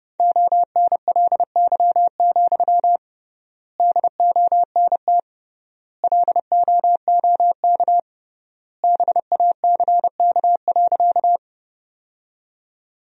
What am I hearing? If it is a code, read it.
Morse code, heard as a keyed tone, about 30 wpm, 700 Hz. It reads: ONLY, DONT LOOK BACK.
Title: 3x Word-Spacing - Sentences from Top 300 Words - 30wpm